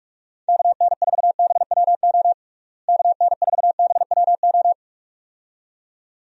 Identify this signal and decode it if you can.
Morse code, heard as a keyed tone, about 46 wpm, 700 Hz. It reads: KN4BWO KN4BWO
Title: USA Call Signs - 6 Characters - Repeated 2x - 45wpm